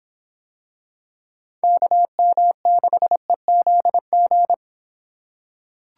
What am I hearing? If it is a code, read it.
Morse code, heard as a keyed tone, about 24 wpm, 700 Hz. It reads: KM6EZG